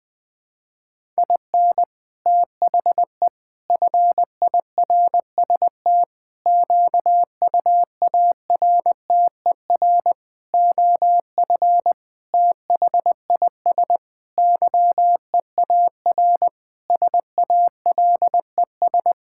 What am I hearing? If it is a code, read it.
Morse code, heard as a keyed tone, about 20 wpm, 700 Hz. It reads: IN THE FIRST QUARTER OF THIS YEAR SALES
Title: Sentences with 7-Word Phrases - 20wpm